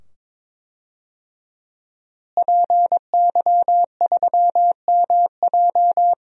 Morse code, heard as a keyed tone, about 22 wpm, 700 Hz. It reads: PY3MJ
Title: DXCC-100 Call Signs - 5 Characters - Repeated 2x - 22wpm